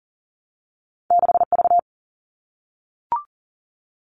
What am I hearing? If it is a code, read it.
Morse code, heard as a keyed tone, about 41 wpm, 700 Hz. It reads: 6V
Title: All Cut Numbers - 2-Digit 40wpm